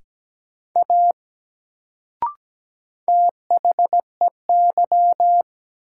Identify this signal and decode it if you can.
Morse code, heard as a keyed tone, about 17 wpm, 700 Hz. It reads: A THEY
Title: Sentences from Top 100 Words - New Word Drill - 17wpm